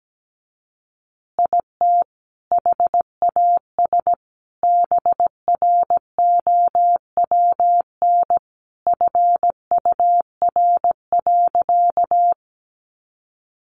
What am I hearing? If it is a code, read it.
Morse code, heard as a keyed tone, about 17 wpm, 700 Hz. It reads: IT HAS BROWN FUR.
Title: Easy Phrases 17wpm